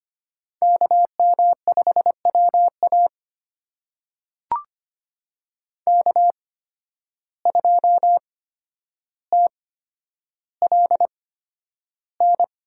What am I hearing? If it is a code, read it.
Morse code, heard as a keyed tone, about 25 wpm, spaced as for 11 wpm, 700 Hz. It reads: KM5WA K2TLN
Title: USA Call Signs - 8x Character Spacing - 25wpm